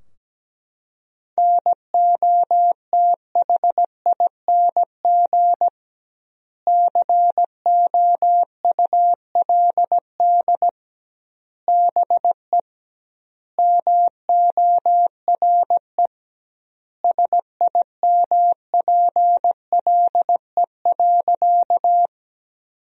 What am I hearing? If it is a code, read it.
Morse code, heard as a keyed tone, about 17 wpm, 700 Hz. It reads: NOTHING COULD BE MORE SIMPLE.
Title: 2x Word-Spacing - Sentences from Top 500 Words - 17wpm